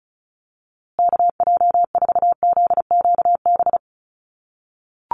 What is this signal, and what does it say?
Morse code, heard as a keyed tone, about 34 wpm, 700 Hz. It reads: KJ4ZQB
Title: USA Call Signs Repeated 2x 35wpm